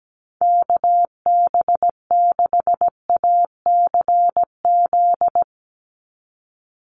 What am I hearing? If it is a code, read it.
Morse code, heard as a keyed tone, about 17 wpm, 700 Hz. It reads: KB6ACZ
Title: USA Call Signs 17wpm